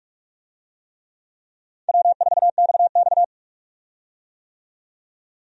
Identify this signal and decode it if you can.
Morse code, heard as a keyed tone, about 42 wpm, 700 Hz. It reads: W4XX
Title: USA Call Signs - 4 Characters - 45wpm